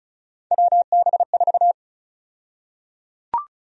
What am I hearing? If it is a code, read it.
Morse code, heard as a keyed tone, about 34 wpm, 700 Hz. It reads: WB4
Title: USA Call Sign Prefixes - 35wpm